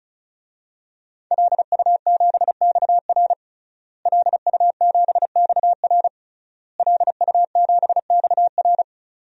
Morse code, heard as a keyed tone, about 34 wpm, 700 Hz. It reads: LU7XR LU7XR LU7XR
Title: DXCC-100 Call Signs - 5 Characters - Repeated 3x - 35wpm